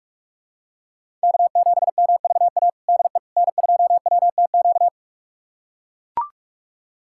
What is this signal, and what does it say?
Morse code, heard as a keyed tone, about 42 wpm, 700 Hz. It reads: K7MVA DE N2WTQ
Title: USA Callsign DE Callsign - 5 Characters - 45wpm